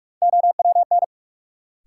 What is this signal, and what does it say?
Morse code, heard as a keyed tone, about 47 wpm, 700 Hz. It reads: OWN
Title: Sets of 2 Words - New Word Drill - 45wpm